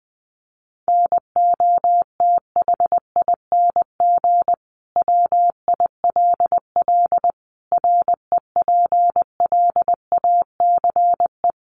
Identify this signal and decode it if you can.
Morse code, heard as a keyed tone, about 20 wpm, 700 Hz. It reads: NOTHING WILL REPLACE
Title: Sentences from Top 1400 Words 20wpm